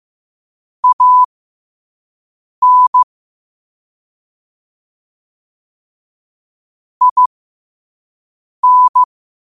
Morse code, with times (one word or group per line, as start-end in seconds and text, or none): 0.8-3.0 AN
7.0-9.0 IN